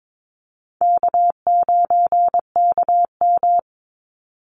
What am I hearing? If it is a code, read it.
Morse code, heard as a keyed tone, about 22 wpm, 700 Hz. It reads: K9KM
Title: Easy Call Signs Repeated 3x - 22wpm